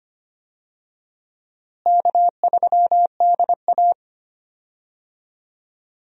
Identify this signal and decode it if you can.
Morse code, heard as a keyed tone, about 25 wpm, 700 Hz. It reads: K3DA
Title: USA Call Signs - 4 Characters - Repeated 3x - 25wpm